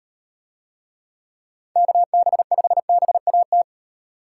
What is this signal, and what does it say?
Morse code, heard as a keyed tone, about 38 wpm, 700 Hz. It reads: KB5BAT